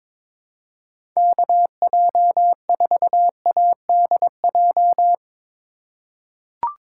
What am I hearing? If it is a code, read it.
Morse code, heard as a keyed tone, about 22 wpm, 700 Hz. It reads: KJ4ADJ
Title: USA Call Signs - 6 Characters - 2x Character Spacing - 22wpm